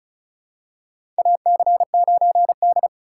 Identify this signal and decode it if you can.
Morse code, heard as a keyed tone, about 34 wpm, 700 Hz. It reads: AC9D
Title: USA Call Signs - 4 Characters - 4x Character Spacing - 35wpm